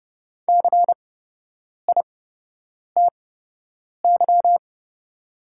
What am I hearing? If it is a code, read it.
Morse code, heard as a keyed tone, about 30 wpm, spaced as for 8 wpm, 700 Hz. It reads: CITY
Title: 8x Character Spacing - Top 500 Words - 30wpm